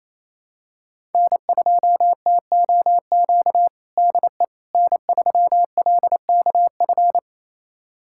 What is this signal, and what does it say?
Morse code, heard as a keyed tone, about 28 wpm, 700 Hz. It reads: N2TOQ DE N3LKF
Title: USA Callsign DE Callsign - 5 Characters - 28wpm